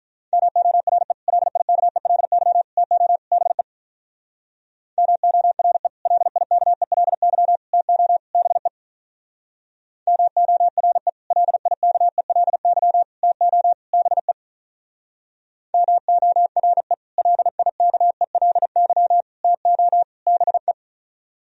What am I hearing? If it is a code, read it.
Morse code, heard as a keyed tone, about 43 wpm, 700 Hz. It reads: MORE LIKELY TO BE MORE LIKELY TO BE MORE LIKELY TO BE MORE LIKELY TO BE
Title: Speed-Racing - Sets of 4 Words 35wpm